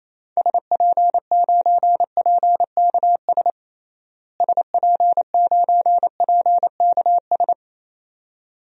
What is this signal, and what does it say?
Morse code, heard as a keyed tone, about 28 wpm, 700 Hz. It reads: SP9PKS SP9PKS
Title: DXCC-100 Call Signs - 6 Characters - Repeated 2x - 28wpm